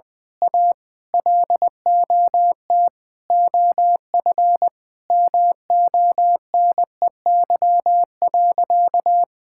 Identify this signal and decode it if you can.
Morse code, heard as a keyed tone, about 20 wpm, 700 Hz. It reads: A LOT OF MONEY.
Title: Sentences from Top 1000 Words - Review Part 6 - 20wpm